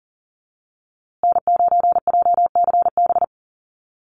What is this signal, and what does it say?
Morse code, heard as a keyed tone, about 40 wpm, 700 Hz. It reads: N9JCB
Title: USA Call Signs Repeated 2x 40wpm